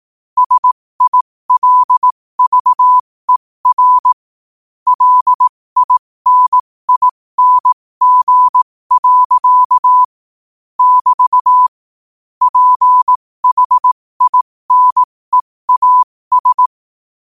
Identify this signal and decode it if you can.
Morse code, heard as a keyed tone, about 18 wpm, spaced as for 15 wpm, 1000 Hz. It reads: SILVER LINING. <BT> PHINEAS
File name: qotdfiles / 15wpm